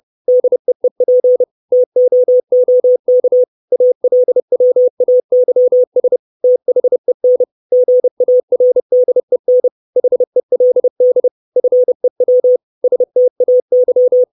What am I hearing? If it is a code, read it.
Morse code, heard as a keyed tone, about 30 wpm, 500 Hz. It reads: DEEP TOOK ALWAYS THEN GARDEN HELD FEW STAY
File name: 632 Words 30WPM